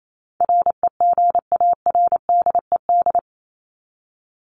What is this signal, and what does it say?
Morse code, heard as a keyed tone, about 28 wpm, 700 Hz. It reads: REGARDED
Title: ED Suffix - Top 100 Words with Base Words 28wpm